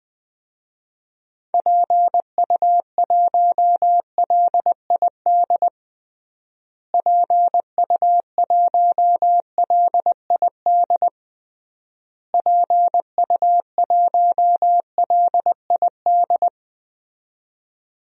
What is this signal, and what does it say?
Morse code, heard as a keyed tone, about 21 wpm, 700 Hz. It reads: PU1LID PU1LID PU1LID
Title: DXCC-100 Call Signs - 6 Characters - Repeated 3x - 20wpm